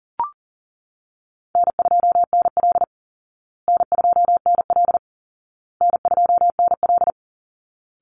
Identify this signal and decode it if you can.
Morse code, heard as a keyed tone, about 40 wpm, 700 Hz. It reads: N2NL N2NL N2NL
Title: Easy Call Signs Repeated 3x - 40wpm